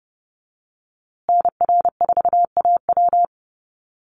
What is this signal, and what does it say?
Morse code, heard as a keyed tone, about 30 wpm, 700 Hz. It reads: NR4AW
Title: USA Call Signs Repeated 3x 30wpm